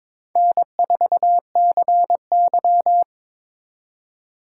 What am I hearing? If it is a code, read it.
Morse code, heard as a keyed tone, about 26 wpm, 700 Hz. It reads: N4CY
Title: Speed Racing - USA 1x2 Call Signs 22wpm